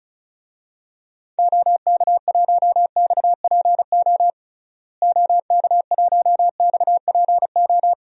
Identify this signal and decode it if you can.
Morse code, heard as a keyed tone, about 34 wpm, 700 Hz. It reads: OK1XPO OK1XPO
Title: DXCC-100 Call Signs - 6 Characters - Repeated 2x - 35wpm